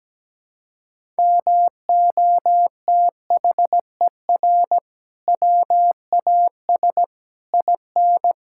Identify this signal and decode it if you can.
Morse code, heard as a keyed tone, about 17 wpm, 700 Hz. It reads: MOTHER WAS IN
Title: Sentences from Top 1000 Words - Review Part 6 - 17wpm